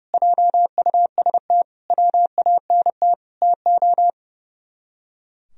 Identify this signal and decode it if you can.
Morse code, heard as a keyed tone, about 34 wpm, 700 Hz. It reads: JUST WANT TO
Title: Speed-Racing - Sets of 4 Words 30wpm